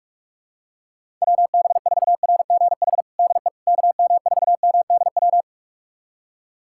Morse code, heard as a keyed tone, about 47 wpm, 700 Hz. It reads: WB4RGH DE KM4MDW
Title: USA Callsign DE Callsign 45wpm